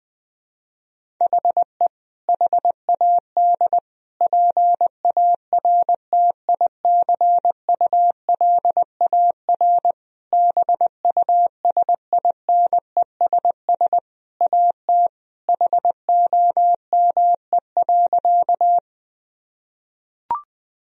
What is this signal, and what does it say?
Morse code, heard as a keyed tone, about 20 wpm, 700 Hz. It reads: HE HAD PARTICULAR BUSINESS AT HOME.
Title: Sentences from Top 1000 Words - Review Part 4 - 20wpm